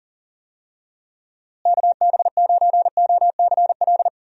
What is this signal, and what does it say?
Morse code, heard as a keyed tone, about 40 wpm, 700 Hz. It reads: KB9OCL
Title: USA Call Signs - 6 Characters - Repeated 3x - 40wpm